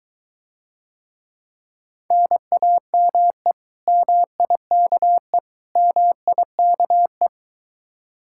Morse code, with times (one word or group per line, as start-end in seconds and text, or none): 2.1-3.5 NAME
3.9-5.4 MIKE
5.8-7.3 MIKE